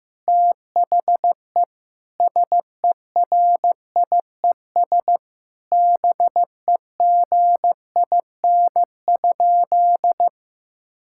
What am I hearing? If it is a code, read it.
Morse code, heard as a keyed tone, about 16 wpm, 700 Hz. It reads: THE SERIES BEGIN?
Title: Sentences from Top 1000 Words - Review Part 5 - 15wpm